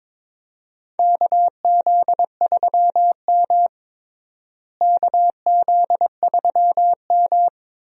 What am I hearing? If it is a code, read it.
Morse code, heard as a keyed tone, about 22 wpm, 700 Hz. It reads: KZ3M KZ3M
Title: USA Call Signs - 4 Characters - Repeated 2x - 22wpm